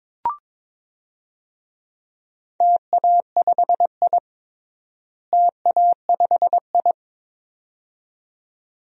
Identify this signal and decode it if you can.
Morse code, heard as a keyed tone, about 22 wpm, 700 Hz. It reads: TA5I TA5I
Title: DXCC-100 Call Signs - 4 Characters - Repeated 2x - 22wpm